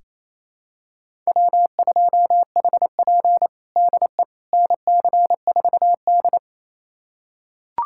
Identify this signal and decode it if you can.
Morse code, heard as a keyed tone, about 27 wpm, 700 Hz. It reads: W2HP DE NC4D
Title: USA Callsign DE Callsign - 4 Characters - 28wpm